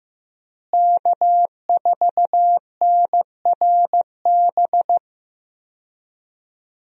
Morse code, heard as a keyed tone, about 15 wpm, 700 Hz. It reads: K4NRB
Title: USA Call Signs - 5 Characters - Repeated 3x - 15wpm